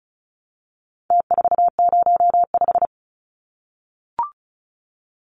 Morse code, heard as a keyed tone, about 34 wpm, 700 Hz. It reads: T405
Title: Numbers - Contest Style - 4-Digit 35wpm